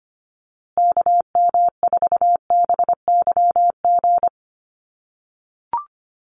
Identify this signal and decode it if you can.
Morse code, heard as a keyed tone, about 25 wpm, 700 Hz. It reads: KM4BYG
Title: USA Call Signs Repeated 3x 25wpm